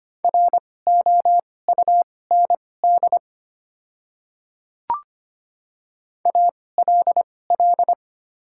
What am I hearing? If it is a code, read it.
Morse code, heard as a keyed tone, about 25 wpm, spaced as for 18 wpm, 700 Hz. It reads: ROUND ALL
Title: 2x Character Spacing - Top 500 Words - 25wpm